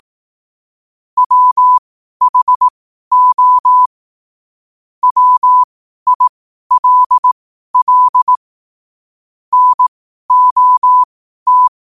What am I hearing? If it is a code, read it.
Morse code, heard as a keyed tone, about 18 wpm, spaced as for 12 wpm, 1000 Hz. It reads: WHO WILL NOT
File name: qotdfiles / 12wpm